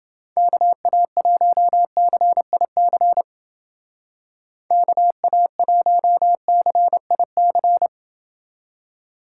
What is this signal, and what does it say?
Morse code, heard as a keyed tone, about 29 wpm, 700 Hz. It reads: KA1CIC KA1CIC